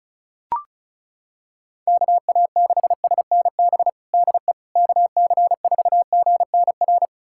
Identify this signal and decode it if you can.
Morse code, heard as a keyed tone, about 34 wpm, 700 Hz. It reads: KA6SNB DE KC4GNR
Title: USA Callsign DE Callsign - 6 Characters - 35wpm